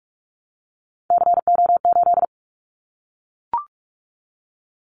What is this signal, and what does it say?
Morse code, heard as a keyed tone, about 46 wpm, 700 Hz. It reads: CO8
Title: Callsign Prefixes for Top 100 DXCC - 45wpm